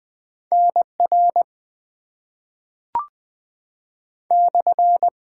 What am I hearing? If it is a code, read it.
Morse code, heard as a keyed tone, about 20 wpm, 700 Hz. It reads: NR /
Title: Top 90 QSO Elements - 6x Character Spacing - 20wpm